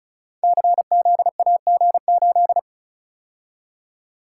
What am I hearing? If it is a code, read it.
Morse code, heard as a keyed tone, about 36 wpm, 690 Hz. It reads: CZAG8
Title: Code Groups - 5 Letter:Numbers - 35wpm